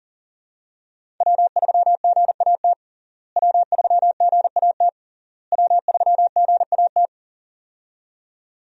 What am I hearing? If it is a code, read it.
Morse code, heard as a keyed tone, about 40 wpm, 700 Hz. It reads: W3GAT W3GAT W3GAT
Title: USA Call Signs - 5 Characters - Repeated 3x - 40wpm